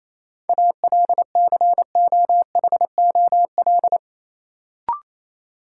Morse code, heard as a keyed tone, about 27 wpm, 700 Hz. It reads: ALCOHOL
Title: Sentences from Top 1400 Words - New Word Drill - 28wpm